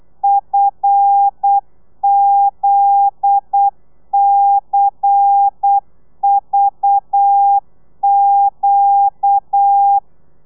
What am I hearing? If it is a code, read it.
Morse code, heard as a keyed tone, about 8 wpm, 800 Hz. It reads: FZCVQ
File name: telegrafia / i2tfj / Lez10 35